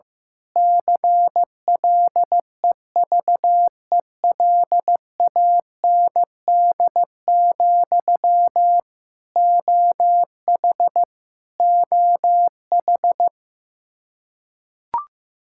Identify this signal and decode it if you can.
Morse code, heard as a keyed tone, about 15 wpm, 700 Hz. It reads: CLEVELAND, OH OH
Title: Practice QSO - Exchange 1 - Beginning - 15wpm